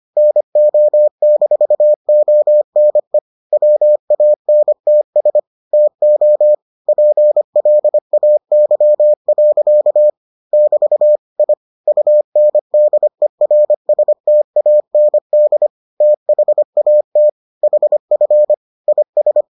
Morse code, heard as a keyed tone, about 25 wpm, 600 Hz. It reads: NO-ONE WANTS TO PLAY. <BT> I UNDERSTAND THAT HF IS